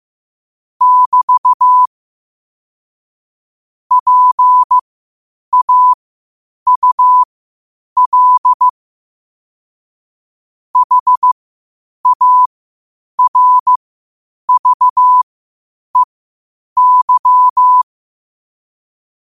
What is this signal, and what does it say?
Morse code, heard as a keyed tone, about 15 wpm, spaced as for 8 wpm, 1000 Hz. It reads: <BT> PAUL HARVEY
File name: qotdfiles / 8wpm